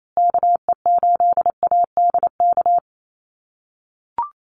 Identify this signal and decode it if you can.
Morse code, heard as a keyed tone, about 28 wpm, 700 Hz. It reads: KE8ADK
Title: USA Call Signs Repeated 2x 28wpm